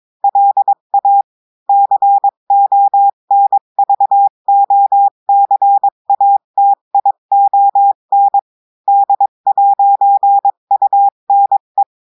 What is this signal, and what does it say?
Morse code, heard as a keyed tone, about 22 wpm, 800 Hz. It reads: LA CONVOCATION D'UNE
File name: cours-du-lundi files / cours-1580-texte-clair-20-mpm